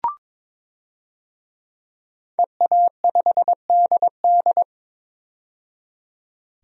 Morse code, heard as a keyed tone, about 22 wpm, 700 Hz. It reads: EA5DD
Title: DXCC-100 Call Signs - 5 Characters - 22wpm